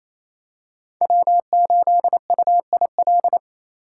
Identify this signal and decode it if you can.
Morse code, heard as a keyed tone, about 27 wpm, 700 Hz. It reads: W8UIL
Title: K1USN SST QSO Practice - 28wpm